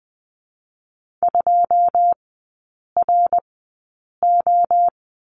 Morse code, heard as a keyed tone, about 20 wpm, spaced as for 8 wpm, 700 Hz. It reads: 2RO
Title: 166 Lesson 27e - Number 7 - Three Characters 20wpm